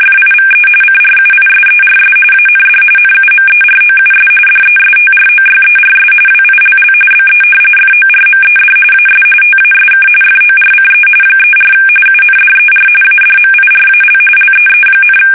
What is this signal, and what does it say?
RTTY, about 45.45 baud. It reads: FIVE CRUISERS AND NUMEROUS DESTROYERS WILL BE FORMED AS TASK FORCE 34 UNDER ADMIRAL LEE,
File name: fsk850 / battleshipnj